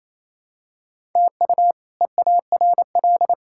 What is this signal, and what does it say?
Morse code, heard as a keyed tone, about 28 wpm, 700 Hz. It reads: TU EARL
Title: K1USN SST QSO Practice - 28wpm